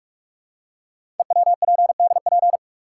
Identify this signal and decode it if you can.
Morse code, heard as a keyed tone, about 44 wpm, 700 Hz. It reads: EWPDP
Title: Code Groups - 5 Letters - 45wpm